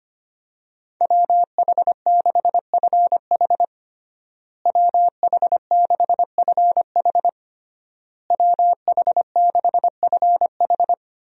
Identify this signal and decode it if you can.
Morse code, heard as a keyed tone, about 25 wpm, 700 Hz. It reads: WH6FH WH6FH WH6FH
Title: USA Call Signs - 5 Characters - Repeated 3x - 25wpm